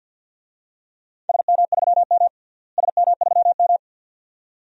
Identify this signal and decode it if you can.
Morse code, heard as a keyed tone, about 53 wpm, 700 Hz. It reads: SM3M SM3M
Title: DXCC-100 Call Signs - 4 Characters - Repeated 2x - 50wpm